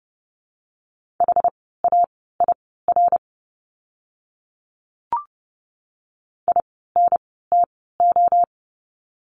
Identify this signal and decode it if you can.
Morse code, heard as a keyed tone, about 30 wpm, spaced as for 17 wpm, 700 Hz. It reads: HAIR INTO
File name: Lesson 17f - Letter u - Words 30wpm